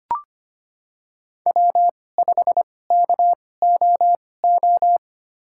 Morse code, heard as a keyed tone, about 25 wpm, spaced as for 18 wpm, 700 Hz. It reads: W5KOO
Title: USA Call Signs - 5 Characters - 2x Character Spacing - 25wpm